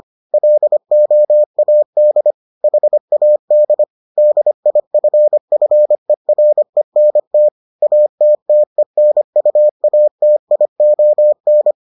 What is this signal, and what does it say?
Morse code, heard as a keyed tone, about 25 wpm, 600 Hz. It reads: LOAD HAD DIFFERENT ATTENUATION